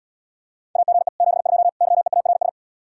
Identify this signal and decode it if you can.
Morse code, heard as a keyed tone, about 79 wpm, 700 Hz. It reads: AGE 72 YEARS